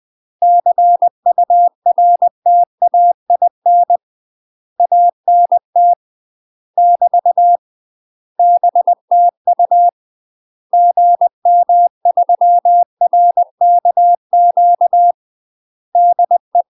Morse code, heard as a keyed tone, about 20 wpm, 700 Hz. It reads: CURTAIN ANT <BT> BTU GM3RKQ DE
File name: UK MASTER QSOs TRAINING / 20WPM / UK MASTER QSOs 0001